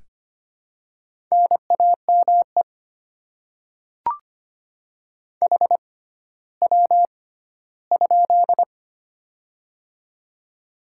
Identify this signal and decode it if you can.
Morse code, heard as a keyed tone, about 25 wpm, spaced as for 14 wpm, 700 Hz. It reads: NAME HW?
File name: Top 20 QSO Elements - 6x Character Spacing - 25wpm